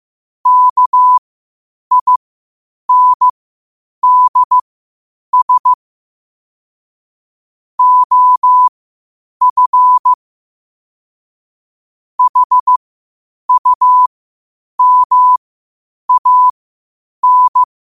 Morse code, heard as a keyed tone, about 15 wpm, spaced as for 8 wpm, 1000 Hz. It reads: KINDS OF HUMAN
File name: qotdfiles / 8wpm